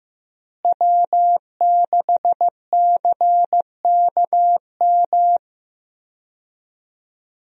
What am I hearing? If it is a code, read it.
Morse code, heard as a keyed tone, about 15 wpm, 700 Hz. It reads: W6CKM
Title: USA Call Signs - 5 Characters - Repeated 3x - 15wpm